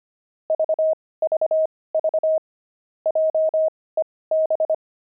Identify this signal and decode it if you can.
Morse code, heard as a keyed tone, about 25 wpm, spaced as for 18 wpm, 630 Hz. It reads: VVV JEB
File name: pangram-fs-18-cs-25-pavement